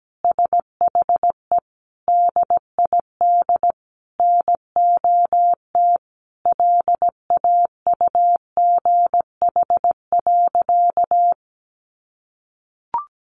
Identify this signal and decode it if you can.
Morse code, heard as a keyed tone, about 17 wpm, 700 Hz. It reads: SHE DID NOT LAUGH.
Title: Sentences from Top 1100 Words 17wpm